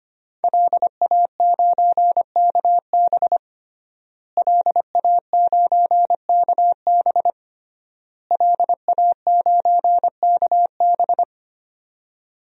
Morse code, heard as a keyed tone, about 25 wpm, 700 Hz. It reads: LA9KB LA9KB LA9KB
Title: DXCC-100 Call Signs - 5 Characters - Repeated 3x - 25wpm